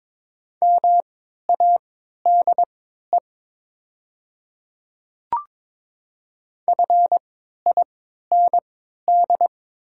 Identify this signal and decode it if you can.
Morse code, heard as a keyed tone, about 23 wpm, spaced as for 13 wpm, 700 Hz. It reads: MADE FIND
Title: 3x Character Spacing - Top 200 Words - 22wpm